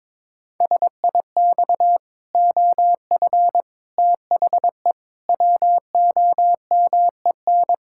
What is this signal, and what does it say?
Morse code, heard as a keyed tone, about 22 wpm, 700 Hz. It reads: SIX OF THE WOMEN
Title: Pangram Challenge v2 - 22wpm